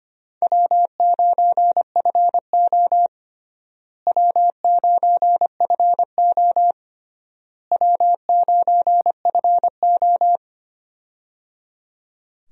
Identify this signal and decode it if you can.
Morse code, heard as a keyed tone, about 25 wpm, 700 Hz. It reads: W9FO W9FO W9FO
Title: USA Call Signs - 4 Characters - Repeated 3x - 25wpm